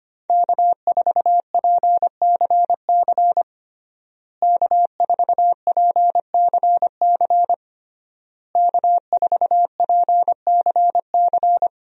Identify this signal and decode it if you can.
Morse code, heard as a keyed tone, about 25 wpm, 700 Hz. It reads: K4PCC K4PCC K4PCC
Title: USA Call Signs - 5 Characters - Repeated 3x - 25wpm